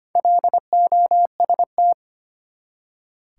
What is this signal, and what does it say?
Morse code, heard as a keyed tone, about 25 wpm, 700 Hz. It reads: LOST
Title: Four Letter Words - 25wpm